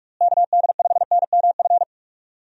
Morse code, heard as a keyed tone, about 41 wpm, 700 Hz. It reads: KD5NMF
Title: USA Call Signs - 6 Characters - Repeated 3x - 45wpm